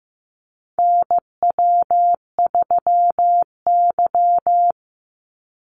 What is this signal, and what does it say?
Morse code, heard as a keyed tone, about 15 wpm, 700 Hz. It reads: NW3Y
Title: Call Signs Repeated 2x - 15wpm